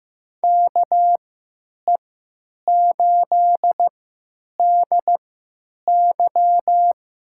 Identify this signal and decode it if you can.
Morse code, heard as a keyed tone, about 15 wpm, spaced as for 9 wpm, 700 Hz. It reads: KE8DY
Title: USA Call Signs - 3x Character Spacing - 15wpm